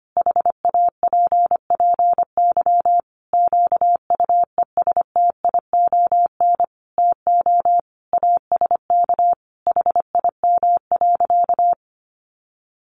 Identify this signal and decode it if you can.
Morse code, heard as a keyed tone, about 25 wpm, 700 Hz. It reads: HAPPY QUESTION TO ASK HIM.
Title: Sentences from Top 1000 Words 25wpm